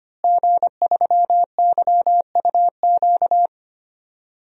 Morse code, heard as a keyed tone, about 25 wpm, 700 Hz. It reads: G3YUQ
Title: DXCC-100 Call Signs - 5 Characters - Repeated 3x - 25wpm